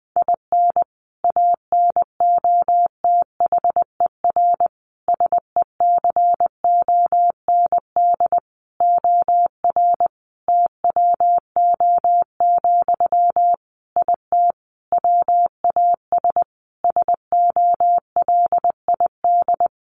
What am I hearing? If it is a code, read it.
Morse code, heard as a keyed tone, about 20 wpm, 700 Hz. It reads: IN ANOTHER SECOND OR TWO, IT WAS SOLID
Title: Sentences from Top 1500 Words 20wpm